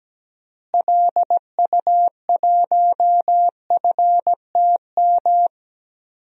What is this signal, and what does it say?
Morse code, heard as a keyed tone, about 17 wpm, 700 Hz. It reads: LU1FTM
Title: DXCC-100 Call Signs - 6 Characters - Repeated 3x - 17wpm